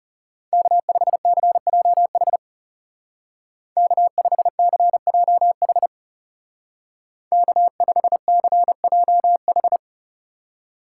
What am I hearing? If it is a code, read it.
Morse code, heard as a keyed tone, about 34 wpm, 700 Hz. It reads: K5CJH K5CJH K5CJH
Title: Speed Racing - USA 1x3 Call Signs 30wpm